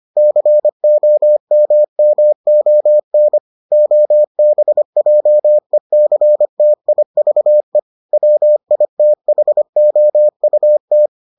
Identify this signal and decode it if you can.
Morse code, heard as a keyed tone, about 25 wpm, 600 Hz. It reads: COMMON OBJECTIVE WITHOUT